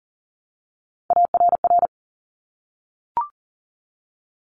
Morse code, heard as a keyed tone, about 41 wpm, 700 Hz. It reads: ARR
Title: Top Three Letter Combinations in Words 40wpm